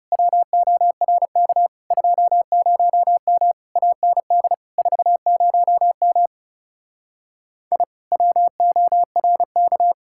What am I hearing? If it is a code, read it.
Morse code, heard as a keyed tone, about 37 wpm, 700 Hz. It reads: WORK 20M AND 40M I WORK
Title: Speed-Racing Common QSO Phrases Encore - 30wpm